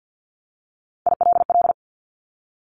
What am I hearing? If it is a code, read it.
Morse code, heard as a keyed tone, about 50 wpm, 700 Hz. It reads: ILL
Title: Top 1751-2000 Words 50wpm